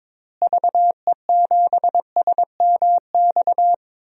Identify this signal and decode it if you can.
Morse code, heard as a keyed tone, about 22 wpm, 700 Hz. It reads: VE7SMX
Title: DXCC-100 Call Signs - 6 Characters - Repeated 2x - 22wpm